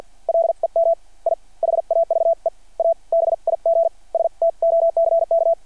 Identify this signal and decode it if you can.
Morse code, heard as a keyed tone, about 52 wpm, spaced as for 43 wpm, 660 Hz. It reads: LEM I HAVE A BIG STOCK
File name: archived / CRSnet